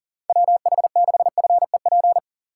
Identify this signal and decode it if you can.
Morse code, heard as a keyed tone, about 37 wpm, 700 Hz. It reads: WH6FEP
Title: USA Call Signs - 6 Characters - 6x Character Spacing - 40wpm